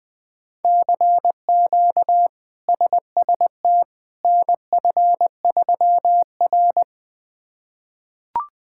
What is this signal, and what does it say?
Morse code, heard as a keyed tone, about 20 wpm, 700 Hz. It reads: CQ SST NF3R
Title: K1USN SST QSO Practice - 20wpm